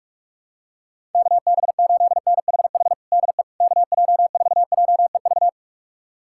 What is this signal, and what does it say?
Morse code, heard as a keyed tone, about 46 wpm, 700 Hz. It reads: KB8NHH DE KJ4JEV
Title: USA Callsign DE Callsign 45wpm